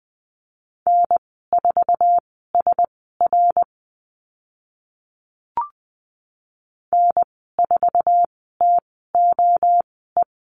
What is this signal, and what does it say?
Morse code, heard as a keyed tone, about 20 wpm, spaced as for 14 wpm, 700 Hz. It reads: N4SR N4TOE
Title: 069 Lesson 12g - Letter d - Call Signs 20wpm